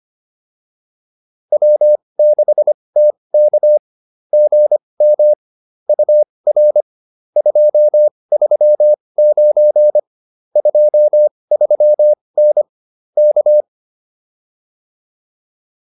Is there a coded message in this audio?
Yes: Morse code from 1.5 to 13.6 s, heard as a keyed tone, about 25 wpm, spaced as for 20 wpm, 600 Hz. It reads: W6TK GM UR 239 23N K